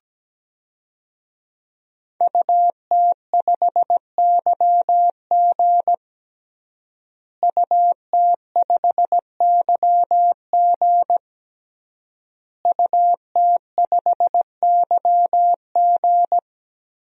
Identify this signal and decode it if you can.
Morse code, heard as a keyed tone, about 17 wpm, 700 Hz. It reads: UT5YG UT5YG UT5YG
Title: DXCC-100 Call Signs - 5 Characters - Repeated 3x - 17wpm